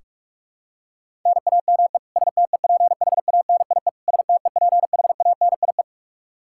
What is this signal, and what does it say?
Morse code, heard as a keyed tone, about 46 wpm, 700 Hz. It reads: NAME STEPHANIE STEPHANIE
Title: Practice QSO - Exchange 1 - Beginning - 45wpm